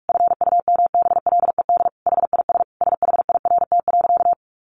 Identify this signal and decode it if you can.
Morse code, heard as a keyed tone, about 46 wpm, 700 Hz. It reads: FUMBLED HIS SHIRT.
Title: A Sound Of Thunder 45wpm